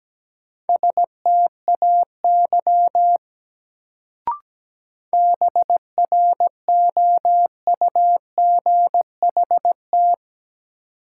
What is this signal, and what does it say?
Morse code, heard as a keyed tone, about 17 wpm, 700 Hz. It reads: STAY BROUGHT
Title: Sentences from Top 500 Words - New Word Drill - 17wpm